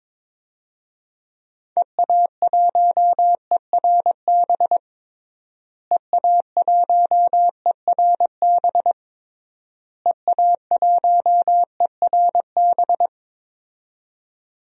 Morse code, heard as a keyed tone, about 21 wpm, 700 Hz. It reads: EA1ERB EA1ERB EA1ERB
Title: DXCC-100 Call Signs - 6 Characters - Repeated 3x - 22wpm